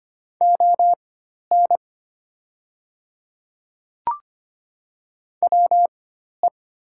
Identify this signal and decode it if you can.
Morse code, heard as a keyed tone, about 25 wpm, spaced as for 12 wpm, 700 Hz. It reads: ON WE
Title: 4x Character Spacing - Top 100 Words - 25wpm